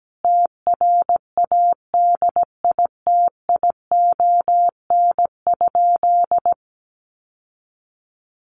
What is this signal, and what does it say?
Morse code, heard as a keyed tone, about 17 wpm, 700 Hz. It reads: TRADITION?
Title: Sentences from Top 1500 Words 17wpm